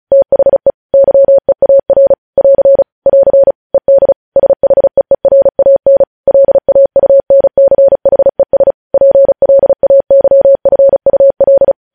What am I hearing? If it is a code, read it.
Morse code, heard as a keyed tone, about 36 wpm, 550 Hz. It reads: THE YEAR <AR> <AR> ED SHEERAN LAUNCHES PLAYFUL
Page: news headlines in morse code at 35 wpm